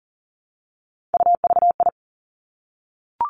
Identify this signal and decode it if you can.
Morse code, heard as a keyed tone, about 38 wpm, 700 Hz. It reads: UVI
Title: Three Challenging Letter-Number Combinations 40wpm